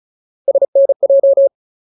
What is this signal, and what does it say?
Morse code, heard as a keyed tone, about 34 wpm, 550 Hz. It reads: SNJ
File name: Ultimate Word List / 35WPM Ultimate